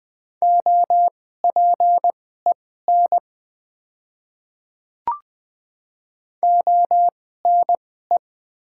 Morse code, heard as a keyed tone, about 20 wpm, spaced as for 15 wpm, 690 Hz. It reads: OPEN ONE
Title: 2x Character Spacing - Top 500 Words - 20wpm